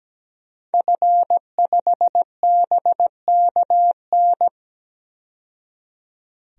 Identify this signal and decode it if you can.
Morse code, heard as a keyed tone, about 17 wpm, 700 Hz. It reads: F5BKN
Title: DXCC-100 Call Signs - 5 Characters - 17wpm